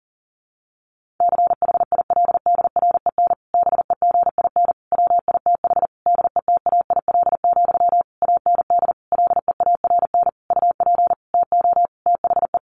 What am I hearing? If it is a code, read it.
Morse code, heard as a keyed tone, about 40 wpm, 700 Hz. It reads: CHILDREN BEGIN WITH DETAIL, AND LEARN UP TO THE
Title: Sentences from Top 1500 Words 40wpm